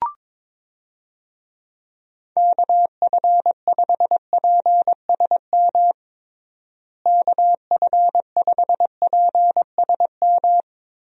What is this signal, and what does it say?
Morse code, heard as a keyed tone, about 22 wpm, 700 Hz. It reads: KF5PSM KF5PSM
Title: USA Call Signs - 6 Characters - Repeated 2x - 22wpm